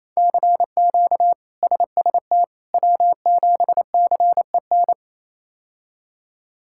Morse code, heard as a keyed tone, about 28 wpm, 700 Hz. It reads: CQ SST W7CEN
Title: K1USN SST QSO Practice - 28wpm